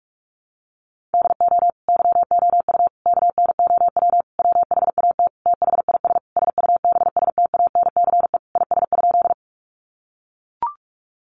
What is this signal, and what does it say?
Morse code, heard as a keyed tone, about 44 wpm, 700 Hz. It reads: DO YOU KNOW WHAT THIS SUBSTANCE IS?
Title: Sentences from Top 1500 Words 45wpm